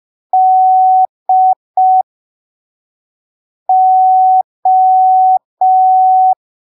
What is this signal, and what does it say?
Morse code, heard as a keyed tone, about 5 wpm, spaced as for 3 wpm, 750 Hz. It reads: DO
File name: arquivos / texto / texto 18 05ppm